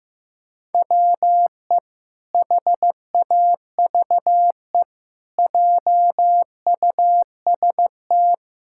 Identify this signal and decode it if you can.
Morse code, heard as a keyed tone, about 15 wpm, 700 Hz. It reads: WE HAVE JUST
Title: Pangram Challenge v2 - 15wpm